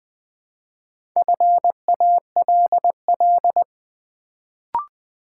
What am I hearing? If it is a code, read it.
Morse code, heard as a keyed tone, about 20 wpm, 700 Hz. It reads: FALL
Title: Build A Word - 1 Letter At A Time - 4-Letter Words - 20wpm